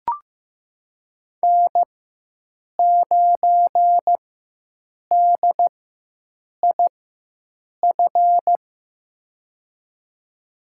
Morse code, heard as a keyed tone, about 15 wpm, spaced as for 7 wpm, 700 Hz. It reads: N9DIF
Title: USA Call Signs - 5 Characters - 4x Character Spacing - 15wpm